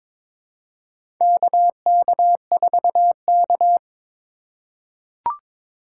Morse code, heard as a keyed tone, about 22 wpm, 700 Hz. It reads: KK4K
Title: USA Call Signs - 4 Characters - 8x Character Spacing - 22wpm